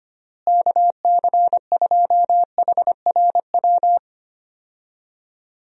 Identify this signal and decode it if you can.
Morse code, heard as a keyed tone, about 25 wpm, 700 Hz. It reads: KC2HRW
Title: USA Call Signs - 6 Characters - Repeated 3x - 25wpm